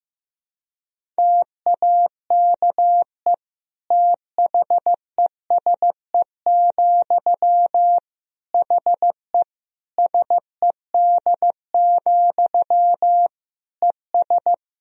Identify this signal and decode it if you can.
Morse code, heard as a keyed tone, about 15 wpm, 700 Hz. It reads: TAKE THESE, HE SED, ES
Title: Sentences from Top 300 Words and Abbreviations 15wpm